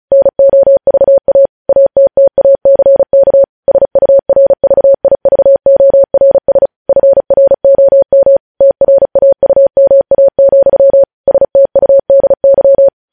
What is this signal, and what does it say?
Morse code, heard as a keyed tone, about 36 wpm, 550 Hz. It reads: NOVA ATTACK SURVIVORS FROM TRAUMA, STUDY
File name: podcast / News headlines-1x-35x35